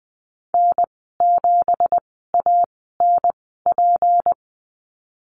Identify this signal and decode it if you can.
Morse code, heard as a keyed tone, about 20 wpm, spaced as for 15 wpm, 700 Hz. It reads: N7ANP
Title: 186 Lesson 30g - Letter b - Call Signs 20wpm